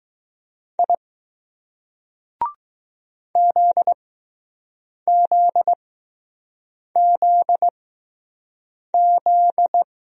Morse code, heard as a keyed tone, about 20 wpm, spaced as for 6 wpm, 700 Hz. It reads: I ZZZZ